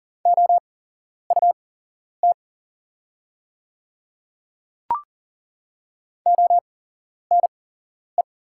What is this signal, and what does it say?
Morse code, heard as a keyed tone, about 40 wpm, spaced as for 11 wpm, 700 Hz. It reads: OUT ONE
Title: 8x Character Spacing - Top 100 Words - 40wpm